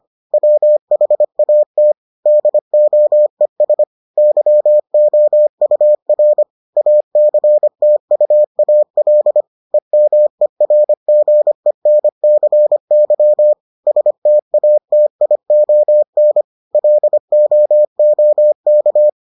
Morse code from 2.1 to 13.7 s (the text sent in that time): DOES YOUR ACTUAL EMERGENCY